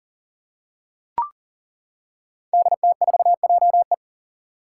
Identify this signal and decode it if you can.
Morse code, heard as a keyed tone, about 38 wpm, 700 Hz. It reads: DT4JE
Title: Five Challenging Letter-Number Combinations 40wpm